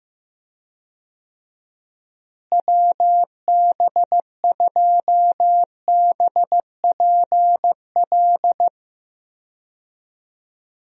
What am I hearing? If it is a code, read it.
Morse code, heard as a keyed tone, about 16 wpm, 700 Hz. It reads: WB2BPL
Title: USA Call Signs - 6 Characters - Repeated 3x - 15wpm